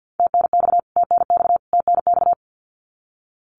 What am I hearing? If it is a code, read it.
Morse code, heard as a keyed tone, about 48 wpm, 700 Hz. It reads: TNX TNX TNX
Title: Abbreviations Repeated 3x - 50wpm